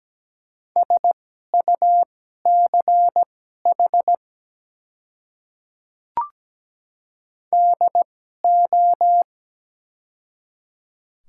Morse code, heard as a keyed tone, about 17 wpm, spaced as for 12 wpm, 700 Hz. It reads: SUCH DO
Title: 2x Character Spacing - Top 200 Words - 17wpm